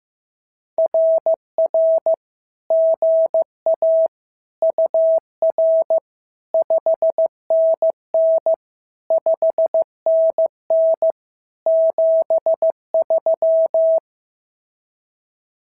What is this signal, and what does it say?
Morse code, heard as a keyed tone, about 14 wpm, 650 Hz. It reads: RR GA UR 5NN 5NN 73
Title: SOTA QSO Practice 15wpm